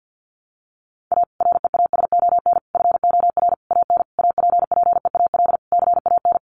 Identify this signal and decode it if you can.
Morse code, heard as a keyed tone, about 50 wpm, 700 Hz. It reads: A REASON FOR AN APPEAL CAN
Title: Sentences from Top 1300 Words 50wpm